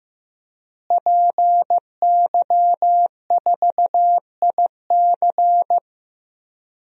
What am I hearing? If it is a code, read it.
Morse code, heard as a keyed tone, about 15 wpm, 700 Hz. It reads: PY4IC
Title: DXCC-100 Call Signs - 5 Characters - Repeated 2x - 15wpm